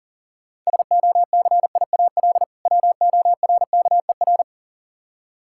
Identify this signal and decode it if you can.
Morse code, heard as a keyed tone, about 40 wpm, 700 Hz. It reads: SOCIAL WORKER
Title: Practice QSO - Exchange 3 - More Info - 40wpm